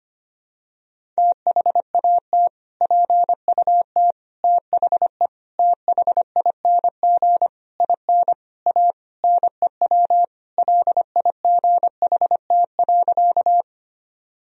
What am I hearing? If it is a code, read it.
Morse code, heard as a keyed tone, about 25 wpm, 700 Hz. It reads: THAT PUT THE THING IN A NEW LIGHT.
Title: Sentences from Top 1000 Words - Review Part 1 - 25wpm